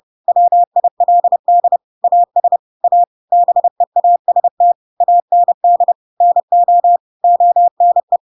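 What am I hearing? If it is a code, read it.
Morse code, heard as a keyed tone, about 30 wpm, 700 Hz. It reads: WILD AS A BEAST AND NO ONE
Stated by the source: Chapter 2 of The Adventures of Pinocchio read as a book in Morse code (CW) at 30 wpm, by ik1hge.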